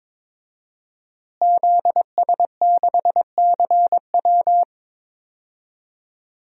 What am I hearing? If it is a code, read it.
Morse code, heard as a keyed tone, about 22 wpm, 700 Hz. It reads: ZS6CW
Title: DXCC-100 Call Signs - 5 Characters - Repeated 2x - 22wpm